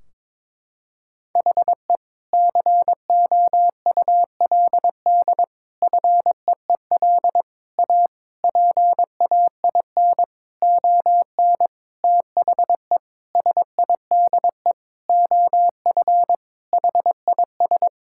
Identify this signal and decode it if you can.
Morse code, heard as a keyed tone, about 22 wpm, 700 Hz. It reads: HE COULD FEEL A PAIN ON THE SIDE OF HIS
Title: Sentences from Top 1600 Words 22wpm